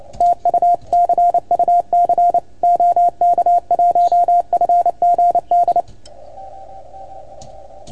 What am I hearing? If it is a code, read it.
Morse code, heard as a keyed tone, about 29 wpm, 690 Hz. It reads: TU CUC OK1FGD
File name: Archiv / CUC310 / OK1FGD CUC 310 pileup